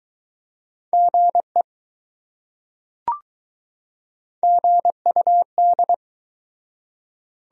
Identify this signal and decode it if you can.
Morse code, heard as a keyed tone, about 20 wpm, 700 Hz. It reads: GE GUD